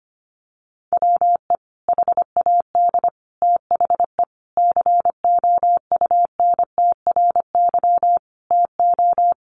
Morse code, heard as a keyed tone, about 25 wpm, 700 Hz. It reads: WE HAD THE COUNTRY TO
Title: Sentences from Top 1100 Words 25wpm